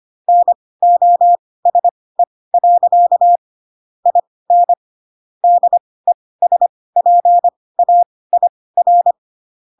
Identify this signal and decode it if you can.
Morse code, heard as a keyed tone, about 25 wpm, spaced as for 18 wpm, 700 Hz. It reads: NOSE. IN DESPAIR
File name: Pinocchio ch3 12 25 ik1hge